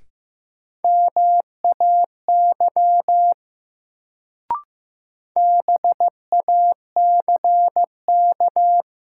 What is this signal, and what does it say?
Morse code, heard as a keyed tone, about 15 wpm, 700 Hz. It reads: MAY BACK
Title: Sets of 2 Words - New Word Drill - 15wpm